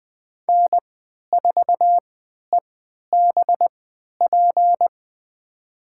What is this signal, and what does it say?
Morse code, heard as a keyed tone, about 21 wpm, spaced as for 12 wpm, 700 Hz. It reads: N4EBP
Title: USA Call Signs - 3x Character Spacing - 20wpm